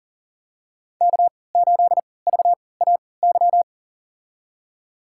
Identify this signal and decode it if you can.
Morse code, heard as a keyed tone, about 40 wpm, spaced as for 23 wpm, 700 Hz. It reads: K8VAY
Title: USA Call Signs - 5 Characters - 3x Character Spacing - 40wpm